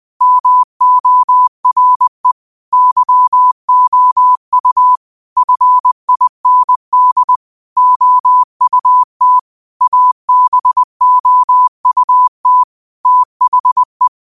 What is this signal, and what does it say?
Morse code, heard as a keyed tone, about 20 wpm, 1000 Hz. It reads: MORE YOU FIND OUT ABOUT THE
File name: qotdfiles / 20wpm